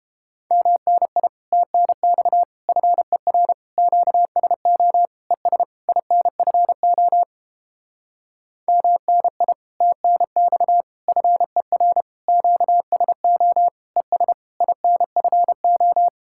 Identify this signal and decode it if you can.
Morse code, heard as a keyed tone, about 29 wpm, 700 Hz. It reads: MNI TNX FER QSO ES INFO MNI TNX FER QSO ES INFO